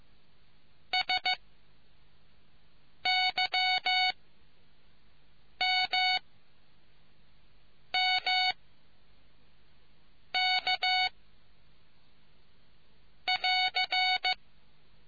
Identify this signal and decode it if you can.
Morse code, heard as a keyed tone, about 15 wpm, 740 Hz. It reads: S Y M M K <AR>